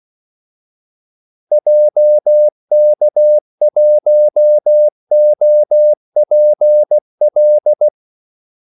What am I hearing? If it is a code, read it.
Morse code, heard as a keyed tone, about 16 wpm, 600 Hz. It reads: JK1OPL
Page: Kaldesignaler 16 wpm